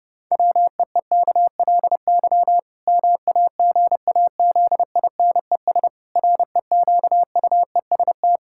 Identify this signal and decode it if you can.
Morse code, heard as a keyed tone, about 32 wpm, 700 Hz. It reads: WEEKLY MAGAZINES REQUEST
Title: Pangram Challenge v2 - 30wpm